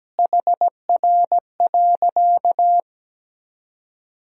Morse code, heard as a keyed tone, about 17 wpm, 700 Hz. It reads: HR.
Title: Sentences from Top 200 Words and Abbreviations 17wpm